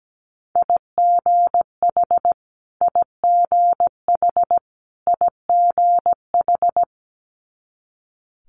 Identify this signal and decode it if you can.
Morse code, heard as a keyed tone, about 17 wpm, 700 Hz. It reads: IGH IGH IGH
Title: Multi-letter Phonemes Repeated 3x 17wpm